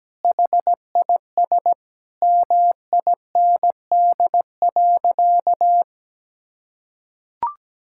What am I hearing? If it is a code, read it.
Morse code, heard as a keyed tone, about 17 wpm, 700 Hz. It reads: HIS MIND.
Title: Sentences from Top 1000 Words - Review Part 5 - 17wpm